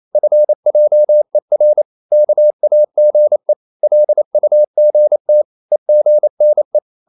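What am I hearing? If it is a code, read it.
Morse code, heard as a keyed tone, about 28 wpm, 600 Hz. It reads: FJER KAGE LUGT EGNE
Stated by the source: Danish short words sent in Morse at 28 wpm, from CW med Gnister.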